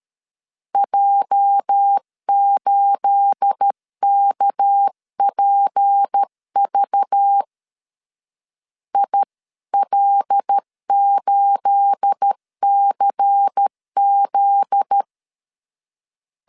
Morse code, heard as a keyed tone, about 13 wpm, 800 Hz. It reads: J8KPV IL8CZ
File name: LEZIONI / 9 28 12wpm 2351